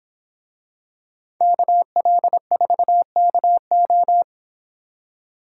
Morse code, heard as a keyed tone, about 25 wpm, 700 Hz. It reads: KL4KO